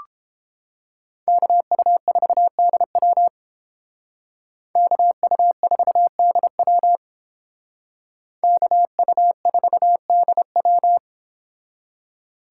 Morse code, heard as a keyed tone, about 28 wpm, 700 Hz. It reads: KU4DW KU4DW KU4DW